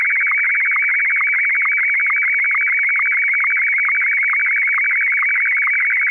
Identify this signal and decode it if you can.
SSTV (slow-scan television), Martin 1.
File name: sstv martin1 32bits